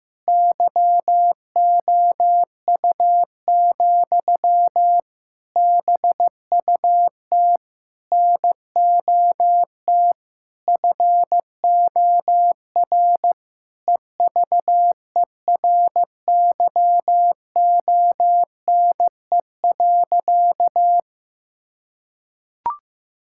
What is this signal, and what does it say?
Morse code, heard as a keyed tone, about 15 wpm, 700 Hz. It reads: YOU, BUT NOT FOR EVERYONE.
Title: Sentences from Top 1000 Words - Review Part 5 - 15wpm